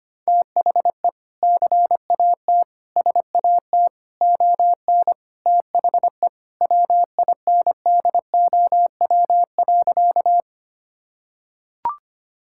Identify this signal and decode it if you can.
Morse code, heard as a keyed tone, about 25 wpm, 700 Hz. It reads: THE CAT SAT ON THE WINDOW.
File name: practice / 25wpm / Sentences with 2-Word Phrases - 25wpm